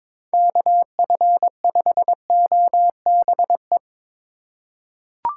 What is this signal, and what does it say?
Morse code, heard as a keyed tone, about 21 wpm, 700 Hz. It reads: KF5OBE
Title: USA Call Signs - 6 Characters - 4x Character Spacing - 22wpm